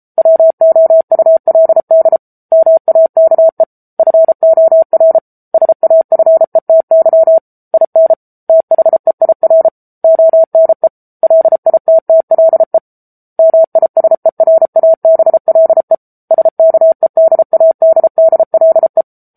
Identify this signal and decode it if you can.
Morse code, heard as a keyed tone, about 34 wpm, 650 Hz. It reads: WOULD MAKE FOR SAFETY IN THEIR ONE LITTLE MISERABLE SKEDADDLE
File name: Book Two / 07-Chapter 7 - 33 WPM